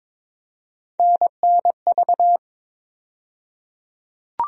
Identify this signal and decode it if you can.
Morse code, heard as a keyed tone, about 22 wpm, 700 Hz. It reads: NNV
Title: USA Call Sign Postfixes - 3 Letters 22wpm